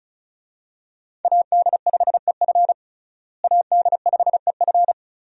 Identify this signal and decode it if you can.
Morse code, heard as a keyed tone, about 36 wpm, 700 Hz. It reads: AD5EF AD5EF
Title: USA Call Signs - 5 Characters - Repeated 2x - 35wpm